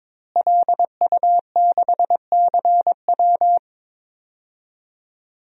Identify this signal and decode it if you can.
Morse code, heard as a keyed tone, about 22 wpm, 700 Hz. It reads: LU6CW
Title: DXCC-100 Call Signs - 5 Characters - Repeated 2x - 22wpm